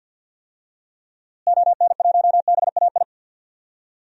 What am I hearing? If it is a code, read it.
Morse code, heard as a keyed tone, about 49 wpm, 700 Hz. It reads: ON1BAI
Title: DXCC-100 Call Signs - 6 Characters - Repeated 2x - 50wpm